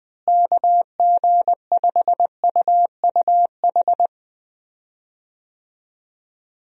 Morse code, heard as a keyed tone, about 20 wpm, 700 Hz. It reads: KG5UUH
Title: USA Call Signs - 6 Characters - Repeated 3x - 20wpm